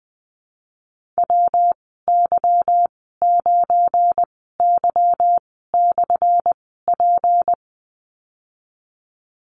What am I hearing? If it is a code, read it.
Morse code, heard as a keyed tone, about 19 wpm, spaced as for 14 wpm, 700 Hz. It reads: WY9Y/P
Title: 179 Lesson 29g - Slash - Call Signs 20wpm